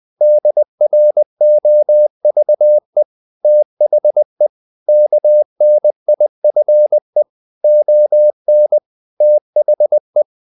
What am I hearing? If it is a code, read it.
Morse code, heard as a keyed tone, about 20 wpm, 600 Hz. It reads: DROVE THE KNIFE ON THE
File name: the old man and the sea part 2 20wpm